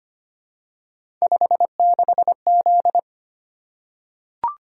Morse code, heard as a keyed tone, about 25 wpm, 700 Hz. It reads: 56Z
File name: Three Letter-Number Combinations - 25wpm